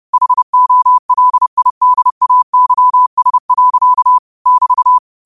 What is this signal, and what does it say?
Morse code, heard as a keyed tone, about 30 wpm, 1000 Hz. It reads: HOLIDAYS. <BT>
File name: qotdfiles / 30wpm